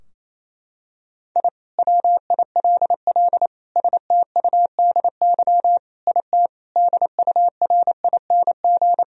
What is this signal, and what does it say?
Morse code, heard as a keyed tone, about 27 wpm, 700 Hz. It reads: I WILL STUDY IT DURING
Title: Sentences from Top 1000 Words - Review Part 2 - 28wpm